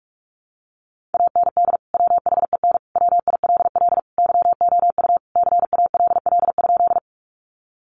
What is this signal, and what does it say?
Morse code, heard as a keyed tone, about 44 wpm, 700 Hz. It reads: AND WHEN WILL YOU CALL?
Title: Sentences from Top 100 Words 45wpm